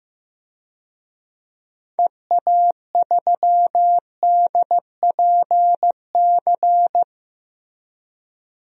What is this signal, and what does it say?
Morse code, heard as a keyed tone, about 15 wpm, 700 Hz. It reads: EA3DPC
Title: DXCC-100 Call Signs - 6 Characters - Repeated 3x - 15wpm